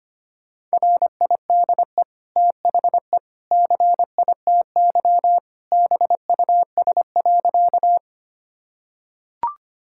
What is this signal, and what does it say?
Morse code, heard as a keyed tone, about 25 wpm, 700 Hz. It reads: RIDE THE CITY BUS.
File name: Easy Sentences with Hints - 25wpm